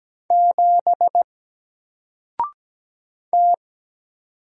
Morse code, heard as a keyed tone, about 17 wpm, 700 Hz. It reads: 7 T
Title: Single Letter-Number Alt 17wpm